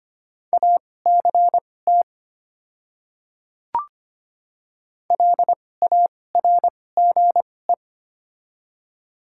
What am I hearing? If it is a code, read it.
Morse code, heard as a keyed tone, about 25 wpm, spaced as for 18 wpm, 700 Hz. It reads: ACT LARGE
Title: 2x Character Spacing - Top 200 Words - 25wpm